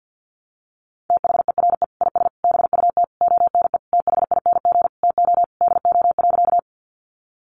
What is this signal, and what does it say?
Morse code, heard as a keyed tone, about 50 wpm, 700 Hz. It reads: THERE IS BUT ONE THING TO DO.
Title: Sentences from Top 100 Words 50wpm